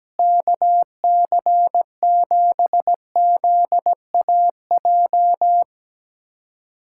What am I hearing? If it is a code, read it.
Morse code, heard as a keyed tone, about 17 wpm, 700 Hz. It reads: KC7ZAJ
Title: USA Callsign DE Callsign - 6 Characters - 17wpm